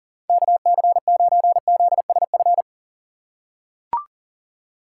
Morse code, heard as a keyed tone, about 40 wpm, 700 Hz. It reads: KC9ZSF
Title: USA Call Signs - 6 Characters - 4x Character Spacing - 40wpm